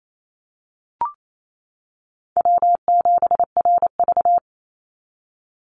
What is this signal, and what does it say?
Morse code, heard as a keyed tone, about 28 wpm, 700 Hz. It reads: W7RV
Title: NA SOTA Chaser Callsigns 28wpm